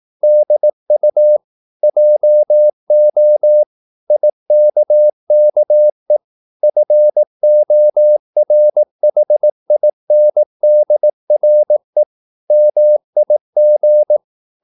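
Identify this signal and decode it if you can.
Morse code, heard as a keyed tone, about 18 wpm, 600 Hz. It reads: DU JO IKKE FORHINDRE MIG
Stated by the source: Slow (QRS) Morse practice at 18 wpm, CW med Gnister.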